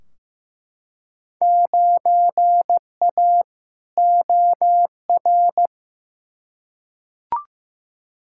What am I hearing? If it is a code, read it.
Morse code, heard as a keyed tone, about 15 wpm, 700 Hz. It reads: 9A OR
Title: Field Day Exchanges Encore - 15wpm